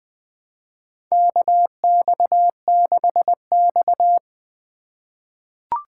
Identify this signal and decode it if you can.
Morse code, heard as a keyed tone, about 20 wpm, 700 Hz. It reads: KX6X
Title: K1USN SST QSO Practice - 20wpm